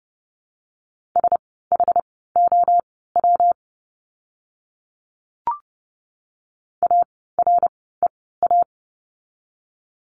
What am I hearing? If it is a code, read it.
Morse code, heard as a keyed tone, about 30 wpm, spaced as for 17 wpm, 700 Hz. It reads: SHOW AREA
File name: course / 30:10wpm / 115 Lesson 19f - Letter w - Words 30wpm